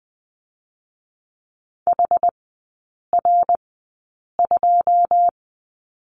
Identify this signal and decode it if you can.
Morse code, heard as a keyed tone, about 21 wpm, spaced as for 8 wpm, 700 Hz. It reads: HR2
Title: 107 Lesson 18e - Letter m - Three Characters 20wpm